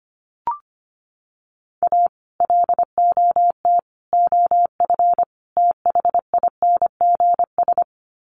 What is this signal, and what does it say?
Morse code, heard as a keyed tone, about 25 wpm, 700 Hz. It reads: A LOT OF THINGS
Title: Sets of 4 Words 25wpm